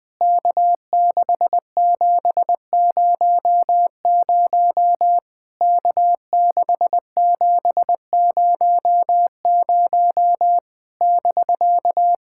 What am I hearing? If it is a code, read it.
Morse code, heard as a keyed tone, about 20 wpm, 700 Hz. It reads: K6700 K6700 <BK>
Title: POTA QSO Practice - 20wpm